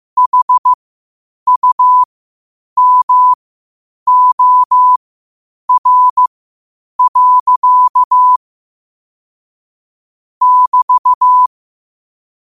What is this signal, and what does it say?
Morse code, heard as a keyed tone, about 15 wpm, spaced as for 8 wpm, 1000 Hz. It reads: HUMOR. <BT>
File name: qotdfiles / 8wpm